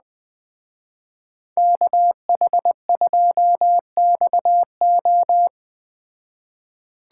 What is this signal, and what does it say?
Morse code, heard as a keyed tone, about 20 wpm, 700 Hz. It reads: KH2XO
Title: Speed Racing - USA 2x2 Call Signs 17wpm